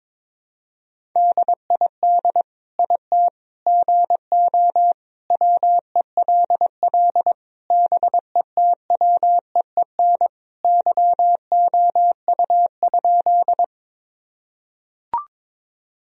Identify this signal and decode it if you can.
Morse code, heard as a keyed tone, about 22 wpm, 700 Hz. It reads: DID IT GO WELL BETWEEN YOU?
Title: Sentences from Top 1000 Words - Review Part 1 - 22wpm